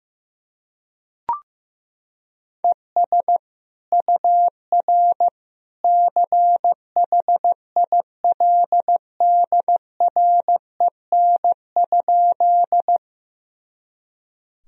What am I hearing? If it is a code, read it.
Morse code, heard as a keyed tone, about 15 wpm, 700 Hz. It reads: ES UR CHILDREN?
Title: Sentences from Top 300 Words and Abbreviations 15wpm